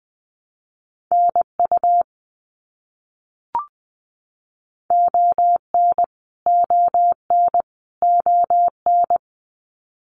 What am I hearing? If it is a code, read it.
Morse code, heard as a keyed tone, about 20 wpm, 700 Hz. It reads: NU ON ON ON
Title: Canadian Provinces and Territories - Repeated 3x - 20wpm